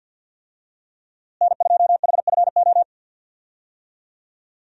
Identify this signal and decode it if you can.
Morse code, heard as a keyed tone, about 52 wpm, 700 Hz. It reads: N2HFO
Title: USA Call Signs - 5 Characters - Repeated 2x - 50wpm